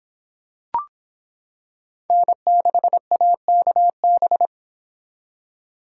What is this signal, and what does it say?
Morse code, heard as a keyed tone, about 26 wpm, 700 Hz. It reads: N6AKB